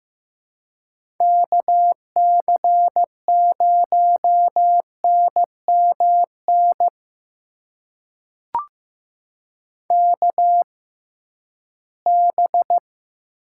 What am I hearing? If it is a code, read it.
Morse code, heard as a keyed tone, about 15 wpm, spaced as for 11 wpm, 700 Hz. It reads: KC0NMN KB
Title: USA Call Signs - 6 Characters - 6x Character Spacing - 15wpm